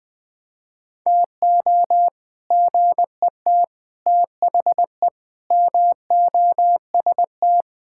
Morse code, heard as a keyed tone, about 20 wpm, 700 Hz. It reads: TO GET THE MOST
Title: Sentences with 7-Word Phrases - 20wpm